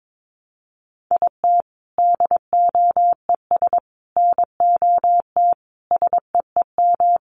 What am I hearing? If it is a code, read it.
Morse code, heard as a keyed tone, about 22 wpm, 700 Hz. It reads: IT DOES NOT SEEM
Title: Sets of 5 Words Encore2 22wpm